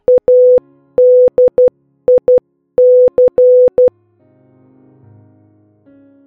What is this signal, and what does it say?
Morse code, heard as a keyed tone, about 12 wpm, 500 Hz. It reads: ADIC